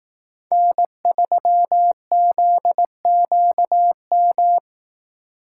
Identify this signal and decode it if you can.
Morse code, heard as a keyed tone, about 18 wpm, 700 Hz. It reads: N3ZQM
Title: Speed Racing - USA 1x3 Call Signs 15wpm